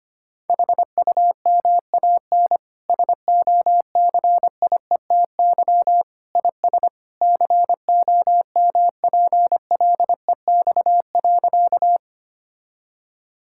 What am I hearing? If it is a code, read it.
Morse code, heard as a keyed tone, about 25 wpm, 700 Hz. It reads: HUMAN SOCIETY IS COMPLEX.
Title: Sentences from Top 1000 Words - Review Part 6 - 25wpm